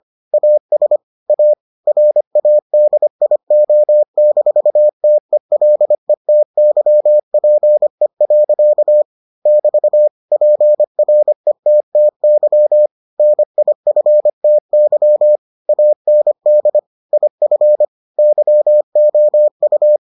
Morse code, heard as a keyed tone, about 25 wpm, 600 Hz. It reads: AS A RADIO-TELETYPE. <BT> PRETTY NIFTY AND IF YOU